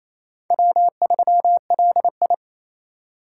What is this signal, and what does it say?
Morse code, heard as a keyed tone, about 27 wpm, 700 Hz. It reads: W3LI
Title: USA Call Signs - 4 Characters - 4x Character Spacing - 28wpm